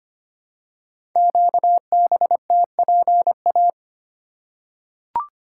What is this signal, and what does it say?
Morse code, heard as a keyed tone, about 25 wpm, 700 Hz. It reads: QBTPA
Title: Code Groups - 5 Letters - 25wpm